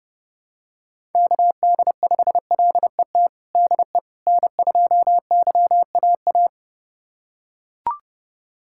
Morse code, heard as a keyed tone, about 30 wpm, 700 Hz. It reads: KD5LET DE N2YAA
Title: USA Callsign DE Callsign 30wpm